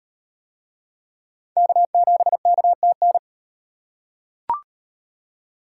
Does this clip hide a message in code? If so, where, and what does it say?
Yes: Morse code from 1.6 to 3.2 s, heard as a keyed tone, about 38 wpm, 700 Hz. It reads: K7KTN